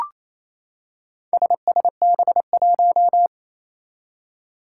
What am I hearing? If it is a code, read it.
Morse code, heard as a keyed tone, about 28 wpm, 700 Hz. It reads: SSB1
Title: Four Challenging Letter-Number Combinations 28wpm